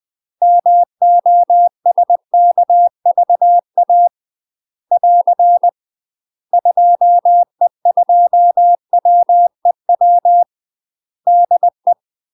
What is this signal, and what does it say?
Morse code, heard as a keyed tone, about 20 wpm, 700 Hz. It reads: MOSKVA <AR> 2E2WEW DE
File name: UK MASTER QSOs TRAINING / 20WPM / UK MASTER QSOs 0015